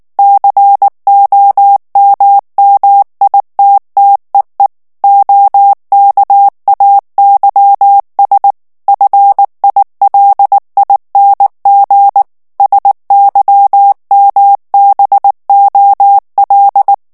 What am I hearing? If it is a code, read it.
Morse code, heard as a keyed tone, about 19 wpm, 800 Hz. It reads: COMMITTEE OKAYS FILING SYMBOL
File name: morse mp3 / ANGLAIS texte clair 3 vitesse 18 mots minute